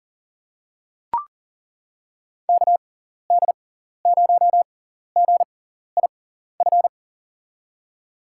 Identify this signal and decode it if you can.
Morse code, heard as a keyed tone, about 40 wpm, spaced as for 14 wpm, 700 Hz. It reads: KD0GIF
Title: USA Call Signs - 6 Characters - 6x Character Spacing - 40wpm